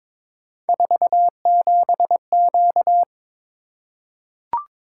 Morse code, heard as a keyed tone, about 22 wpm, 700 Hz. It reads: 47Q
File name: Three Letter-Number Combinations - 22wpm